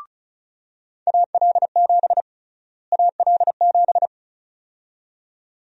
Morse code, heard as a keyed tone, about 34 wpm, 700 Hz. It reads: AL7 AL7
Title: USA Call Sign Prefixes - Repeated 2x - 35wpm